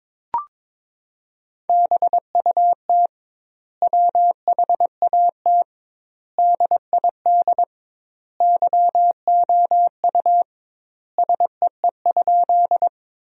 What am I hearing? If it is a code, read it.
Morse code, heard as a keyed tone, about 22 wpm, 700 Hz. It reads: BUT WHAT DID YOU SEE?
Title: 2x Word-Spacing - Sentences from Top 100 Words - 22wpm